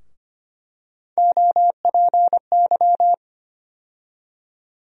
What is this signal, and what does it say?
Morse code, heard as a keyed tone, about 25 wpm, 700 Hz. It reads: OPY
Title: USA Call Sign Postfixes - 3 Letters 25wpm